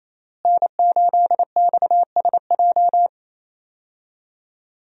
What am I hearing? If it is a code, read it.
Morse code, heard as a keyed tone, about 28 wpm, 700 Hz. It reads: N8XSJ
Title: USA Call Signs - 5 Characters - Repeated 3x - 28wpm